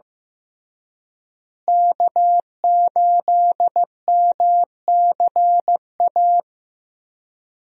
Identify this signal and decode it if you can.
Morse code, heard as a keyed tone, about 16 wpm, 700 Hz. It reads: K8MCA
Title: USA Call Signs - 5 Characters - Repeated 2x - 15wpm